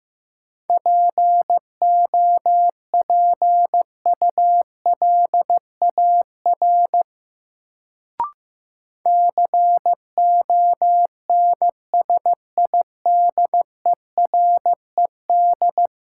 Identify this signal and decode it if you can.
Morse code, heard as a keyed tone, about 15 wpm, 700 Hz. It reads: POPULAR CONSIDERED
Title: Sets of 5 Words - New Word Drill - 15wpm